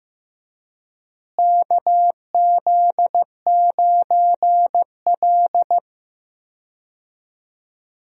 Morse code, heard as a keyed tone, about 15 wpm, 700 Hz. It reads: KZ9L
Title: USA Call Signs - 4 Characters - Repeated 2x - 15wpm